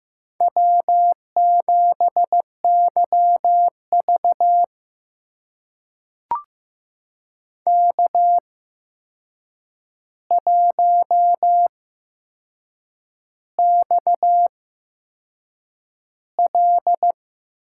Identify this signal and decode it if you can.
Morse code, heard as a keyed tone, about 15 wpm, spaced as for 6 wpm, 700 Hz. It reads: W7YV K1XL
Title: USA Call Signs - 4 Characters - 8x Character Spacing - 15wpm